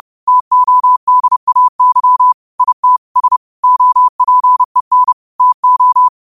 Morse code, heard as a keyed tone, about 30 wpm, 1000 Hz. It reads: TODAY IT S OPEN TO
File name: qotdfiles / 30wpm